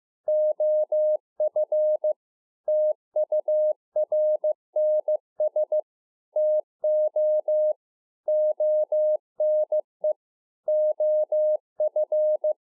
Morse code, heard as a keyed tone, about 15 wpm, 610 Hz. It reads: OF TURNS TO ONE OF
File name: 2011-04-mp3 / A SMALL TRANCEIVER 2 15WPM